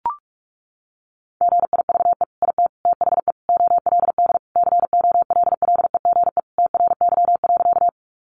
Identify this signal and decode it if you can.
Morse code, heard as a keyed tone, about 44 wpm, 700 Hz. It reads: GIVE IT THE OLD COLLEGE TRY.
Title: Sentences from Top 1300 Words 45wpm